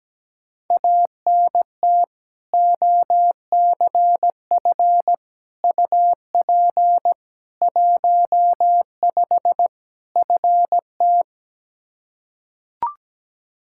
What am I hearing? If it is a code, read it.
Morse code, heard as a keyed tone, about 17 wpm, 700 Hz. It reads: ANT OCF UP 15 FT
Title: Practice QSO - Exchange 2 - Info - 17wpm